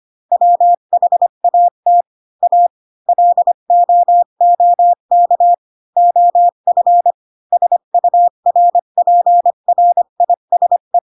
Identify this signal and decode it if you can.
Morse code, heard as a keyed tone, about 25 wpm, 700 Hz. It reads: WHAT A LOOK OF SURPRISE
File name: Pinocchio / Pinocchio ch1 20 25 ik1hge